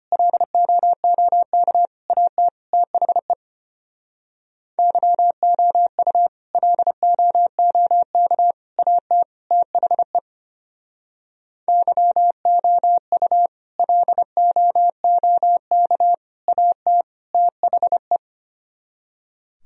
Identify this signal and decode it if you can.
Morse code, heard as a keyed tone, about 31 wpm, 700 Hz. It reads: LOOK AT THE YOU LOOK AT THE YOU LOOK AT THE